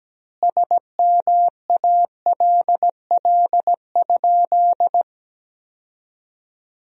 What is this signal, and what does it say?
Morse code, heard as a keyed tone, about 17 wpm, 700 Hz. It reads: SMALL?
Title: 2x Word-Spacing - Sentences from Top 200 Words - 17wpm